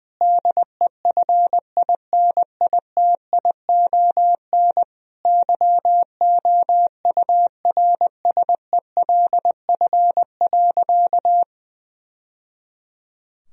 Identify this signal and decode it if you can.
Morse code, heard as a keyed tone, about 20 wpm, 700 Hz. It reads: DEFINITION YOURSELF.
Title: Sentences from Top 1000 Words - Review Part 4 - 20wpm